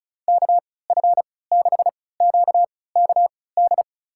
Morse code, heard as a keyed tone, about 35 wpm, spaced as for 20 wpm, 690 Hz. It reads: KF6QKD
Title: USA Call Signs - 6 Characters - 3x Character Spacing - 35wpm